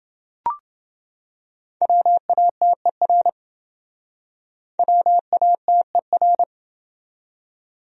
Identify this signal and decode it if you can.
Morse code, heard as a keyed tone, about 28 wpm, 700 Hz. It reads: WATER WATER